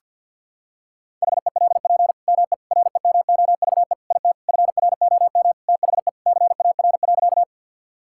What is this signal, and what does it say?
Morse code, heard as a keyed tone, about 50 wpm, 700 Hz. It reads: HELP ME REMOVE IT FROM THE CAR.
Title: Sentences from Top 1000 Words - Review Part 5 - 50wpm